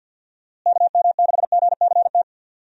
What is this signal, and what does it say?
Morse code, heard as a keyed tone, about 48 wpm, 700 Hz. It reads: KM6GCT
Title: USA Call Signs - 6 Characters - Repeated 2x - 50wpm